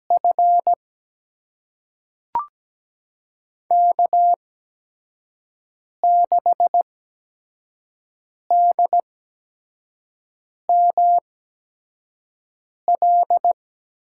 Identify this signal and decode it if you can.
Morse code, heard as a keyed tone, about 17 wpm, spaced as for 5 wpm, 700 Hz. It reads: F K6DML
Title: USA Call Signs - 8x Character Spacing - 17wpm